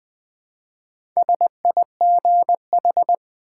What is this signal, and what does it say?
Morse code, heard as a keyed tone, about 20 wpm, 700 Hz. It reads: SIGH
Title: Build A Word - 2 Letters At A Time - 4-Letter Words - 20wpm